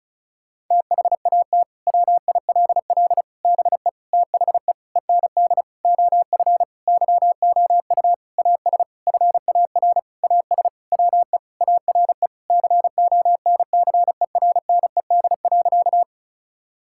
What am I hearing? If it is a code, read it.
Morse code, heard as a keyed tone, about 36 wpm, 700 Hz. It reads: THAT WILL BE THE END OF YOU AS FAR AS WE ARE CONCERNED.
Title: Sentences from Top 1000 Words - Review Part 6 - 35wpm